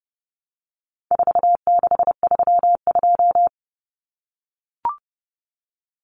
Morse code, heard as a keyed tone, about 30 wpm, 700 Hz. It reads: 4632
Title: Common Cut Numbers - 4-Digit 30wpm